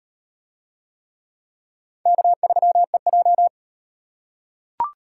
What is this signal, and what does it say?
Morse code, heard as a keyed tone, about 37 wpm, 700 Hz. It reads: K3EJ